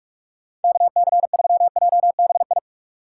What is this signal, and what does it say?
Morse code, heard as a keyed tone, about 46 wpm, 700 Hz. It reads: KC3JBI
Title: USA Call Signs - 6 Characters - 2x Character Spacing - 45wpm